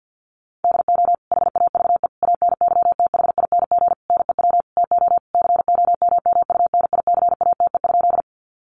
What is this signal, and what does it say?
Morse code, heard as a keyed tone, about 50 wpm, 700 Hz. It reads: DO HAVE ANYTHING NEW TO COMMUNICATE?
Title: Sentences from Top 1300 Words 50wpm